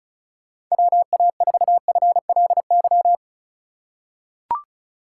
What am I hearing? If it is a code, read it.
Morse code, heard as a keyed tone, about 36 wpm, 700 Hz. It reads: WA4FLY
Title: USA Call Signs - 6 Characters - 3x Character Spacing - 35wpm